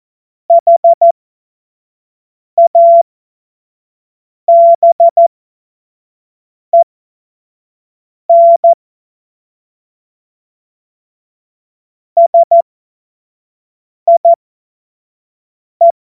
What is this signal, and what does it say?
Morse code, heard as a keyed tone, about 14 wpm, spaced as for 5 wpm, 680 Hz. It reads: HABEN SIE
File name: Morsekurs / Witze / Witz 13. Januar 2026 25 F70